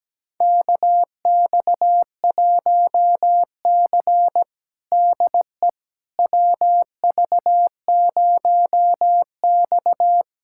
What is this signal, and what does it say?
Morse code, heard as a keyed tone, about 17 wpm, 700 Hz. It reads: KX1C DE WV0X
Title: USA Callsign DE Callsign - 4 Characters - 17wpm